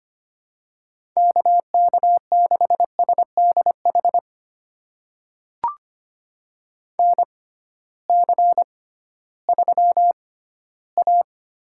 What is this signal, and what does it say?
Morse code, heard as a keyed tone, about 25 wpm, spaced as for 14 wpm, 700 Hz. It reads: KK6SDH NC3A
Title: USA Call Signs - 6x Character Spacing - 25wpm